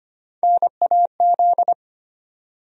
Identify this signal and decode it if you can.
Morse code, heard as a keyed tone, about 25 wpm, 700 Hz. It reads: NAZ
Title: USA Call Sign Postfixes - 3 Letters 25wpm